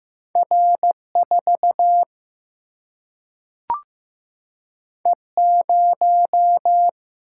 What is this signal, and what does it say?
Morse code, heard as a keyed tone, about 15 wpm, 700 Hz. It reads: R 4 E 0
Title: Single-Letter Single-Number 15wpm